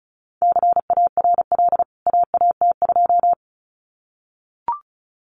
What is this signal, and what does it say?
Morse code, heard as a keyed tone, about 34 wpm, 700 Hz. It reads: CARL AAT2
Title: CWT Exchange 35wpm